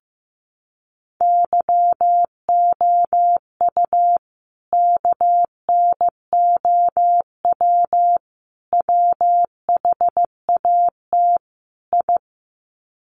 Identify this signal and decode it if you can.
Morse code, heard as a keyed tone, about 15 wpm, 700 Hz. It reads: YOU KNOW WHAT I
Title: Sets of 4 Words 15wpm